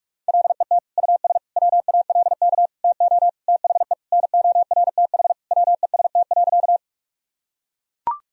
Morse code, heard as a keyed tone, about 41 wpm, 700 Hz. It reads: LET US WALK TO THE NORTH WEST.
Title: Sentences from Top 1000 Words - Review Part 2 - 45wpm